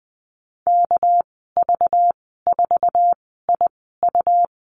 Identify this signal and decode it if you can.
Morse code, heard as a keyed tone, about 20 wpm, spaced as for 14 wpm, 700 Hz. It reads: KV4IU
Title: 207 Lesson 33g - Letter j - Call Signs 20wpm